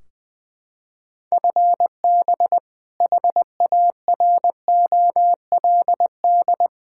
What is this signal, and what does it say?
Morse code, heard as a keyed tone, about 20 wpm, 700 Hz. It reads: FB HAROLD
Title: Practice QSO - Exchange 3 - More Info - 20wpm